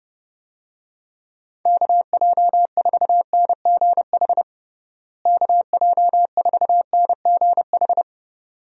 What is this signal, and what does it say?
Morse code, heard as a keyed tone, about 30 wpm, 700 Hz. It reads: KJ4NGH KJ4NGH
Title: USA Call Signs - 6 Characters - Repeated 2x - 30wpm